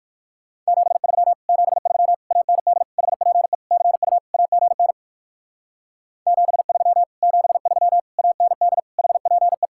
Morse code, heard as a keyed tone, about 59 wpm, 690 Hz. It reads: 73 73 AND HPE CU AGN 73 73 AND HPE